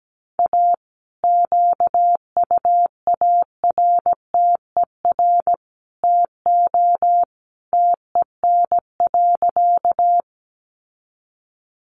Morse code, heard as a keyed tone, about 17 wpm, 700 Hz. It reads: A QUARTER TO TEN.
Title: Sentences from Top 1300 Words 17wpm